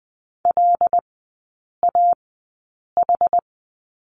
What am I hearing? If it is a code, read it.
Morse code, heard as a keyed tone, about 20 wpm, spaced as for 8 wpm, 700 Hz. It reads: LAH
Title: 100 Lesson 17e - Letter u - Three Characters 20wpm